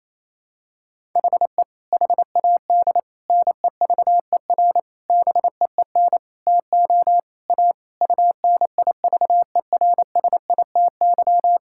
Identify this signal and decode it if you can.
Morse code, heard as a keyed tone, about 28 wpm, 700 Hz. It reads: HE HAD NEVER BEEN TO A UNIVERSITY
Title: Sentences from Top 1000 Words - Review Part 5 - 28wpm